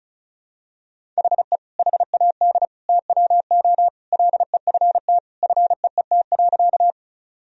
Morse code, heard as a keyed tone, about 34 wpm, 700 Hz. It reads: HE HAD TWO LEFT FEET.
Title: Sentences from Top 1000 Words - Review Part 1 - 35wpm